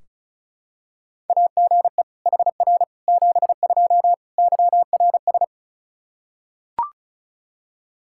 Morse code, heard as a keyed tone, about 34 wpm, 700 Hz. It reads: AGE HR 72 YRS
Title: Practice QSO - Exchange 3 - More Info - 35wpm